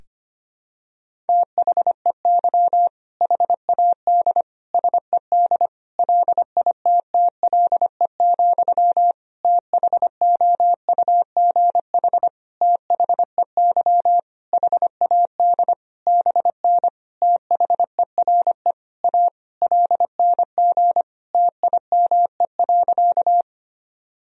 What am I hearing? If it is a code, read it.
Morse code, heard as a keyed tone, about 25 wpm, 700 Hz. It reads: THEY HAD SED LITTLE, THOUGH THEY HAD BN THERE A LNG TIME.
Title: Sentences from Top 400 Words and Abbreviations 25wpm